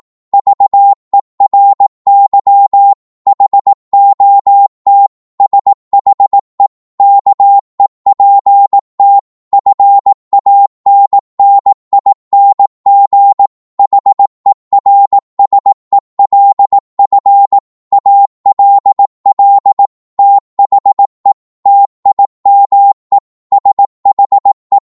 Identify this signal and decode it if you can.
Morse code, heard as a keyed tone, about 18 wpm, 800 Hz. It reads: VERY HOT SHE KEPT FANNING HERSELF ALL THE TIME SHE
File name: Morse / psg-18wpm